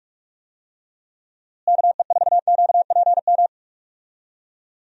Morse code, heard as a keyed tone, about 46 wpm, 700 Hz. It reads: KE4QPM
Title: USA Call Signs - 6 Characters - Repeated 3x - 45wpm